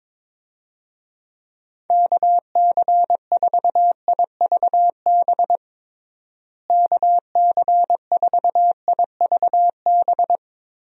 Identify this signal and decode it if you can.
Morse code, heard as a keyed tone, about 22 wpm, 700 Hz. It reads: KC4IVB KC4IVB
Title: USA Call Signs - 6 Characters - Repeated 3x - 22wpm